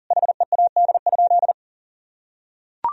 Morse code, heard as a keyed tone, about 37 wpm, 700 Hz. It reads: HEAD?
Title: Sentences from Top 1000 Words - Review Part 1 - 40wpm